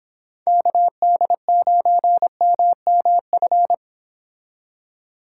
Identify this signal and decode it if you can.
Morse code, heard as a keyed tone, about 30 wpm, 700 Hz. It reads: KD9MMF